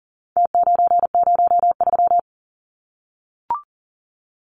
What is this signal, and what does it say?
Morse code, heard as a keyed tone, about 40 wpm, 700 Hz. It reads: T903
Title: Numbers - Contest Style - 4-Digit 40wpm